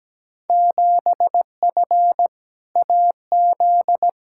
Morse code, heard as a keyed tone, about 17 wpm, 700 Hz. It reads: 7F AZ
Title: Field Day Exchanges Encore - 17wpm